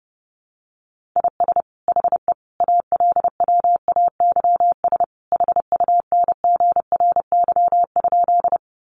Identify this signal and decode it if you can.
Morse code, heard as a keyed tone, about 30 wpm, 700 Hz. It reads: IS HE ALWAYS HUNGRY?
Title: Sentences from Top 1400 Words 30wpm